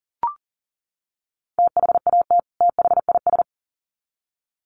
Morse code, heard as a keyed tone, about 40 wpm, 700 Hz. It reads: THAT THIS
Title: Sets of 2 Words 40wpm